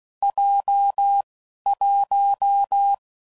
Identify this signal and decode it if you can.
Morse code, heard as a keyed tone, about 16 wpm, 800 Hz. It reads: J 1
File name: Combinations / Difficult Combos / WPJ WPJ1